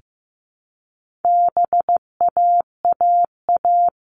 Morse code, heard as a keyed tone, about 15 wpm, 700 Hz. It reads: BAAA
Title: All Cut Numbers - 4-Digit 15wpm